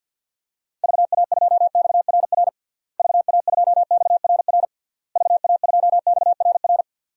Morse code, heard as a keyed tone, about 53 wpm, 700 Hz. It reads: VA2XRR VA2XRR VA2XRR
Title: DXCC-100 Call Signs - 6 Characters - Repeated 3x - 50wpm